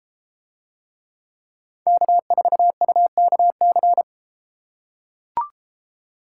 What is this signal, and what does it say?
Morse code, heard as a keyed tone, about 30 wpm, 700 Hz. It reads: K4UKC